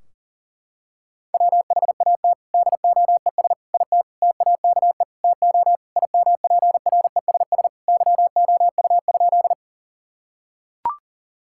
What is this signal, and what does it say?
Morse code, heard as a keyed tone, about 40 wpm, 700 Hz. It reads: WHAT DOES IT TAKE TO IMPRESS YOU?
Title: Sentences from Top 1600 Words 40wpm